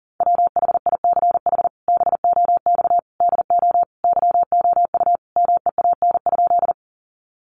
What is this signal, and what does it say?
Morse code, heard as a keyed tone, about 40 wpm, 700 Hz. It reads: WHICH BOX DO YOU MEAN?
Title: Sentences from Top 500 Words 40wpm